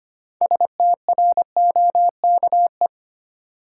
Morse code, heard as a keyed tone, about 25 wpm, 700 Hz. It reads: STROKE
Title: Six Letter Words - 25wpm